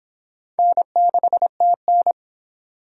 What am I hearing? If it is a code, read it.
Morse code, heard as a keyed tone, about 25 wpm, 700 Hz. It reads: N6TN